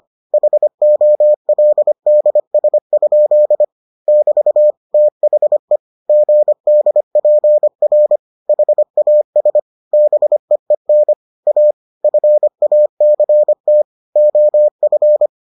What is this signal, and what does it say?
Morse code, heard as a keyed tone, about 25 wpm, 600 Hz. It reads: HOLDS? <BT> THE GDPR HAS BEEN A FACT OF